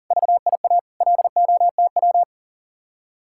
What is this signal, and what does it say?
Morse code, heard as a keyed tone, about 49 wpm, 700 Hz. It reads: VIA LOTW
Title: Speed-Racing Common QSO Phrases - 40wpm